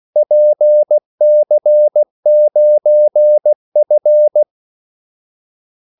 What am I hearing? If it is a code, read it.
Morse code, heard as a keyed tone, about 16 wpm, 600 Hz. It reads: PC9F
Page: Kaldesignaler 16 wpm